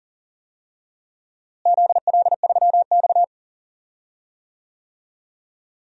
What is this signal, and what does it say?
Morse code, heard as a keyed tone, about 40 wpm, 700 Hz. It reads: ZL3X
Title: DXCC-100 Call Signs - 4 Characters - Repeated 2x - 40wpm